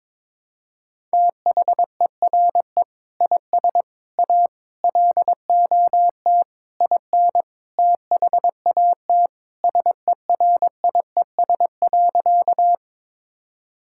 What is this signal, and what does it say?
Morse code, heard as a keyed tone, about 21 wpm, 700 Hz. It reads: THERE IS A LOT IN THAT SERIES.
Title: Sentences from Top 1000 Words - Review Part 5 - 22wpm